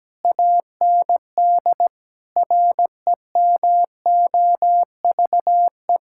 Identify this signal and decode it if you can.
Morse code, heard as a keyed tone, about 17 wpm, 700 Hz. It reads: AND REMOVE
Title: Sentences from Top 1000 Words - Review Part 5 - 17wpm